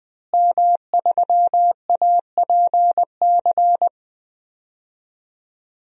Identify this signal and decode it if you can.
Morse code, heard as a keyed tone, about 21 wpm, 700 Hz. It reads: M3APC
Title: DXCC-100 Call Signs - 5 Characters - Repeated 3x - 20wpm